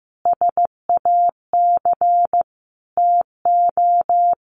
Morse code, heard as a keyed tone, about 15 wpm, 700 Hz. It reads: SAC TO
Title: A Sound Of Thunder 15wpm